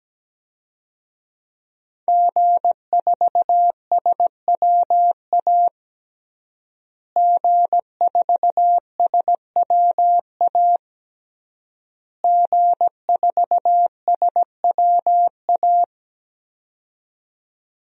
Morse code, heard as a keyed tone, about 17 wpm, 700 Hz. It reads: G4SWA G4SWA G4SWA
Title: DXCC-100 Call Signs - 5 Characters - Repeated 3x - 17wpm